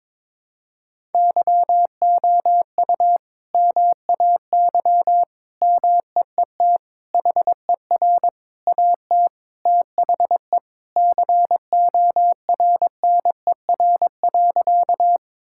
Morse code, heard as a keyed tone, about 22 wpm, 700 Hz. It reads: YOU MAY MEET HER AT THE CORNER.
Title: Sentences from Top 1600 Words 22wpm